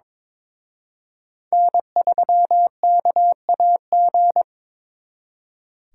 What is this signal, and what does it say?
Morse code, heard as a keyed tone, about 26 wpm, 700 Hz. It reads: N3KAG
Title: Speed Racing - USA 1x3 Call Signs 22wpm